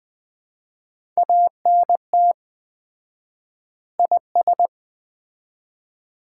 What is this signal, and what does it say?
Morse code, heard as a keyed tone, about 20 wpm, 700 Hz. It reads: ANT IS
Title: Common QSO Phrases - 4x Word-Spacing - 20wpm